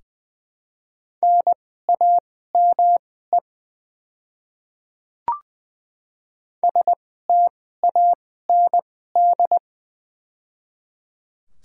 Morse code, heard as a keyed tone, about 21 wpm, spaced as for 15 wpm, 700 Hz. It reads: NAME STAND
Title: 2x Character Spacing - Top 300 Words - 20wpm